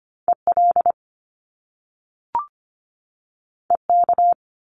Morse code, heard as a keyed tone, about 25 wpm, 700 Hz. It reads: EL EK
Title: Top Two Letter Combinations in Words 25wpm